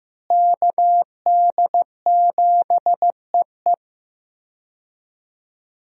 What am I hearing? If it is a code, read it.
Morse code, heard as a keyed tone, about 15 wpm, 700 Hz. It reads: KD7EE
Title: USA Call Signs - 5 Characters - Repeated 2x - 15wpm